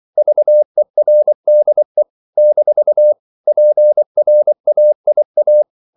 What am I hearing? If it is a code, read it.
Morse code, heard as a keyed tone, about 24 wpm, 600 Hz. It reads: VERDE - PRAIA
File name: CW / Practice / NationalCapitals24